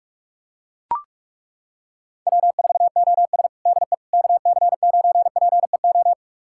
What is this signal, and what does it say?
Morse code, heard as a keyed tone, about 42 wpm, 700 Hz. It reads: W4OS DE KC9PEO
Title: USA Callsign DE Callsign 45wpm